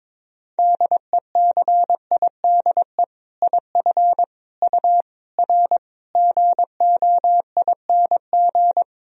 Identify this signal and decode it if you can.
Morse code, heard as a keyed tone, about 22 wpm, 700 Hz. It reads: DECIDE IF U R GOING
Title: Sentences from Top 500 Words and Abbreviations 22wpm